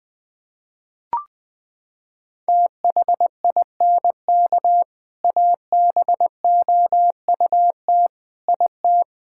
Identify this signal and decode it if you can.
Morse code, heard as a keyed tone, about 20 wpm, 700 Hz. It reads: THINK ABOUT IT
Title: Sentences with 3-Word Phrases - 20wpm